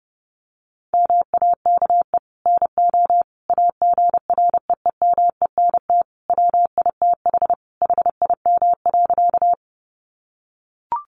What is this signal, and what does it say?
Morse code, heard as a keyed tone, about 30 wpm, 700 Hz. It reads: MAKE NO AGREEMENT WITH HIM.
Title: Sentences from Top 1500 Words 30wpm